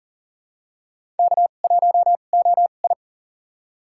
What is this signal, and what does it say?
Morse code, heard as a keyed tone, about 40 wpm, spaced as for 29 wpm, 700 Hz. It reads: K1OI
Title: USA Call Signs - 2x Character Spacing - 40wpm